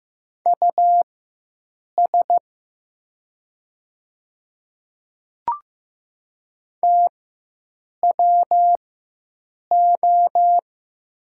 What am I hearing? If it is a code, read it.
Morse code, heard as a keyed tone, about 15 wpm, spaced as for 7 wpm, 700 Hz. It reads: US TWO
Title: 4x Character Spacing - Top 300 Words - 15wpm